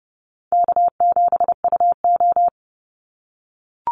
Morse code, heard as a keyed tone, about 30 wpm, 700 Hz. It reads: K7UO
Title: USA Call Signs 30wpm